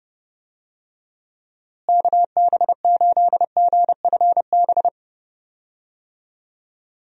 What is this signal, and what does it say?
Morse code, heard as a keyed tone, about 30 wpm, 700 Hz. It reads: KB8GFB
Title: USA Call Signs - 6 Characters - Repeated 2x - 30wpm